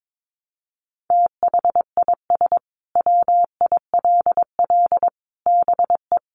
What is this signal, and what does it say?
Morse code, heard as a keyed tone, about 22 wpm, 700 Hz. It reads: THIS WILL BE
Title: Sentences from Top 1200 Words 22wpm